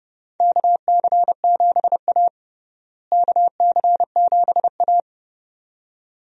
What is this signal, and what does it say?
Morse code, heard as a keyed tone, about 32 wpm, 700 Hz. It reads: KC7A KC7A
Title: USA Call Signs - 4 Characters - Repeated 2x - 30wpm